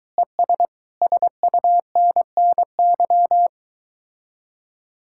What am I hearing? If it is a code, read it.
Morse code, heard as a keyed tone, about 23 wpm, 700 Hz. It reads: ES SUNNY